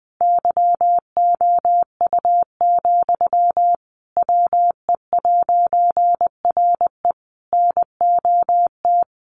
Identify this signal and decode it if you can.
Morse code, heard as a keyed tone, about 20 wpm, 700 Hz. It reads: YOU, WE'RE NOT
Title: A Sound Of Thunder 20wpm